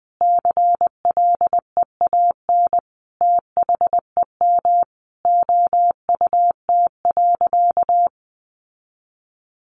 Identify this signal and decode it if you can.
Morse code, heard as a keyed tone, about 20 wpm, 700 Hz. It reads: CLEAN THEM OUT.
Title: Sentences from Top 1500 Words 20wpm